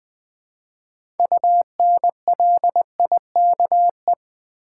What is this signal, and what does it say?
Morse code, heard as a keyed tone, about 20 wpm, 700 Hz. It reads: UNLIKE
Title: 6 Letter Words - First 3 Letters Given - 20wpm